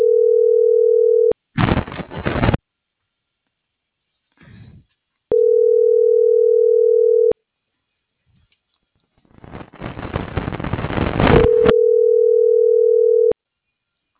Ringback tone (North America), 440 + 480 Hz.